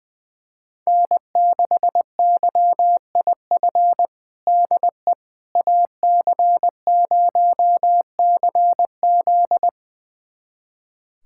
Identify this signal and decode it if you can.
Morse code, heard as a keyed tone, about 21 wpm, 700 Hz. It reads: N6YIF DE AC0CZ
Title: USA Callsign DE Callsign - 5 Characters - 20wpm